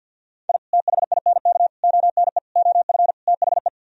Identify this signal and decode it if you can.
Morse code, heard as a keyed tone, about 64 wpm, 700 Hz. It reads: I THINK ONE OF THE